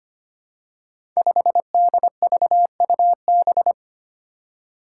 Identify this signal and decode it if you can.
Morse code, heard as a keyed tone, about 25 wpm, 700 Hz. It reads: 5DVUB
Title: Five Challenging Letter-Number Combinations 25wpm